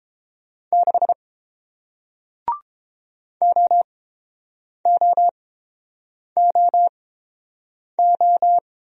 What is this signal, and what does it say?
Morse code, heard as a keyed tone, about 30 wpm, spaced as for 7 wpm, 700 Hz. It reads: 6 OOOO